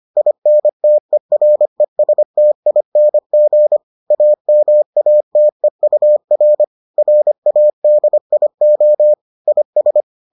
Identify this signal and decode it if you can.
Morse code, heard as a keyed tone, about 25 wpm, 600 Hz. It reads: INTERESTING AMATEUR RADIO IS